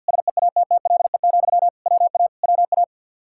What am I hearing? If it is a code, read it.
Morse code, heard as a keyed tone, about 52 wpm, 700 Hz. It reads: SEATTLE, WA WA
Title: Practice QSO - Exchange 1 - Beginning - 50wpm